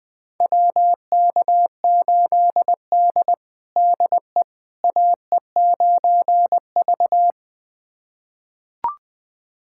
Morse code, heard as a keyed tone, about 20 wpm, 700 Hz. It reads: WK8D DE AE9V
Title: USA Callsign DE Callsign - 4 Characters - 20wpm